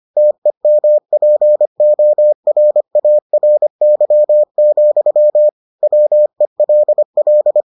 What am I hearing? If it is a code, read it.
Morse code, heard as a keyed tone, about 25 wpm, 600 Hz. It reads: TEMPORARY, WELL